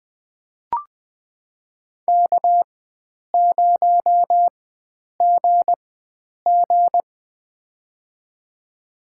Morse code, heard as a keyed tone, about 20 wpm, spaced as for 9 wpm, 700 Hz. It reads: K0GG
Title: USA Call Signs - 4 Characters - 4x Character Spacing - 20wpm